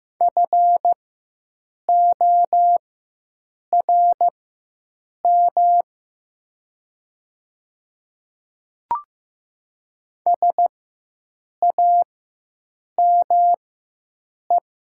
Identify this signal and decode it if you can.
Morse code, heard as a keyed tone, about 15 wpm, spaced as for 7 wpm, 700 Hz. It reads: FORM SAME
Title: 4x Character Spacing - Top 300 Words - 15wpm